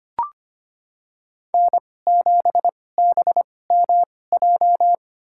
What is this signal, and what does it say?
Morse code, heard as a keyed tone, about 25 wpm, spaced as for 18 wpm, 700 Hz. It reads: N7BMJ
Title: USA Call Signs - 5 Characters - 2x Character Spacing - 25wpm